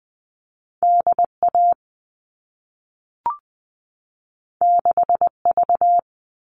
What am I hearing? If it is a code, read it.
Morse code, heard as a keyed tone, about 20 wpm, 700 Hz. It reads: DA 6V
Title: All Cut Numbers - 2-Digit 20wpm